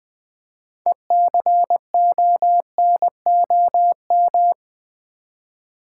Morse code, heard as a keyed tone, about 20 wpm, 700 Hz. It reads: ECONOM
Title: Build A Word - 3 Letters At A Time - 9-Letter Words - 20wpm